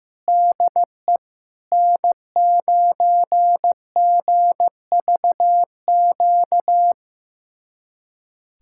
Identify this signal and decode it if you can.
Morse code, heard as a keyed tone, about 15 wpm, 700 Hz. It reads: DE N9GVQ
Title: USA Callsign DE Callsign 15wpm